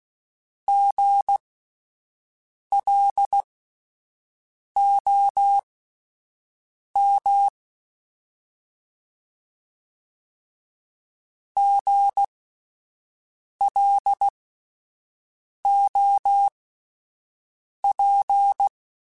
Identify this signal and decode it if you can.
Morse code, heard as a keyed tone, about 16 wpm, spaced as for 5 wpm, 800 Hz. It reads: GLOM GLOP